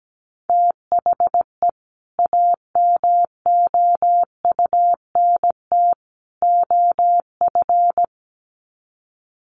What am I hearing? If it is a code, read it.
Morse code, heard as a keyed tone, about 17 wpm, 700 Hz. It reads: THE AMOUNT OF
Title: Sets of 3 Words 17wpm